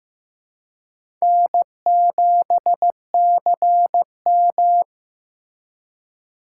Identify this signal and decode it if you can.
Morse code, heard as a keyed tone, about 15 wpm, 700 Hz. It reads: N7CM
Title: USA Call Signs - 4 Characters - 2x Character Spacing - 15wpm